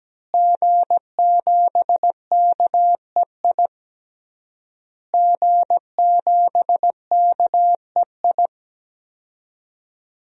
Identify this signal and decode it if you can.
Morse code, heard as a keyed tone, about 17 wpm, 700 Hz. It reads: G7KEI G7KEI
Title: DXCC-100 Call Signs - 5 Characters - Repeated 3x - 17wpm